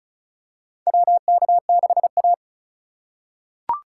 Morse code, heard as a keyed tone, about 34 wpm, 700 Hz. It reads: WK6A
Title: USA Call Signs - 4 Characters - 2x Character Spacing - 35wpm